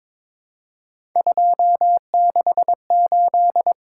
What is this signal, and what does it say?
Morse code, heard as a keyed tone, about 22 wpm, 700 Hz. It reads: 268
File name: practice / 22wpm / Three Letter-Number Combinations - 22wpm